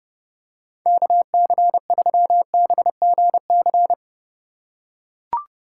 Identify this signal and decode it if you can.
Morse code, heard as a keyed tone, about 31 wpm, 700 Hz. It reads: KC3BGC